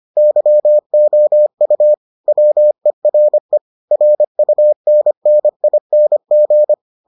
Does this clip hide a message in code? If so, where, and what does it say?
Yes: Morse code from 0.2 to 6.7 s, heard as a keyed tone, about 25 wpm, 600 Hz. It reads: YOU WERE RUNNING